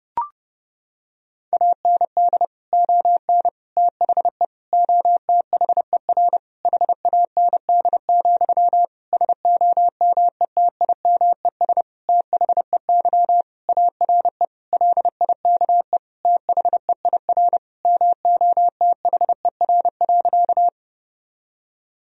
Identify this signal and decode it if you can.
Morse code, heard as a keyed tone, about 32 wpm, 700 Hz. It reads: AND ON THE OTHER HAND, SOMETIMES THEY ARE LIKE THEIR MOTHER.
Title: Sentences from Top 1000 Words - Review Part 3 - 30wpm